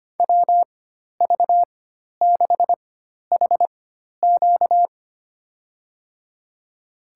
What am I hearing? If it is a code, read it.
Morse code, heard as a keyed tone, about 25 wpm, spaced as for 12 wpm, 700 Hz. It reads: WV6HQ
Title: USA Call Signs - 5 Characters - 4x Character Spacing - 25wpm